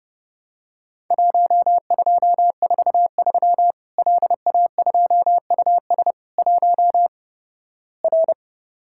Morse code, heard as a keyed tone, about 30 wpm, 700 Hz. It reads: 1243 LA2US 1 R
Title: Worked All Europe DX Contest - QTC Exchange - 30wpm